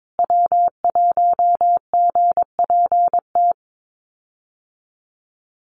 Morse code, heard as a keyed tone, about 22 wpm, 700 Hz. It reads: W1GPT
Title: USA Call Signs Repeated 3x 22wpm